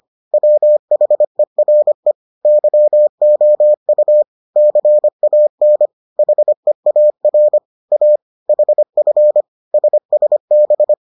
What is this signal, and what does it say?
Morse code, heard as a keyed tone, about 25 wpm, 600 Hz. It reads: WHERE YOU CAN HEAR A HF SSB